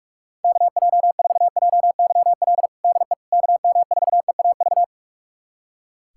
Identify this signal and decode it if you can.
Morse code, heard as a keyed tone, about 44 wpm, 690 Hz. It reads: KJ4JYL DE KM4EAV
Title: USA Callsign DE Callsign 45wpm